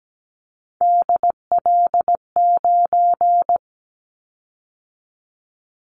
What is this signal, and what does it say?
Morse code, heard as a keyed tone, about 17 wpm, 700 Hz. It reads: DL9
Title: Callsign Prefixes for Top 50 DXCC - 17wpm